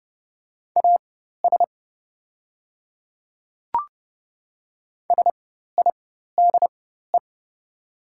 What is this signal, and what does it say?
Morse code, heard as a keyed tone, about 30 wpm, spaced as for 14 wpm, 700 Hz. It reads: AS SIDE
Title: 4x Character Spacing - Top 300 Words - 30wpm